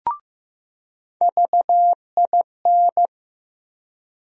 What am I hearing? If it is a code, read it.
Morse code, heard as a keyed tone, about 15 wpm, 700 Hz. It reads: VIN
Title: USA Call Sign Postfixes - 3 Letters 15wpm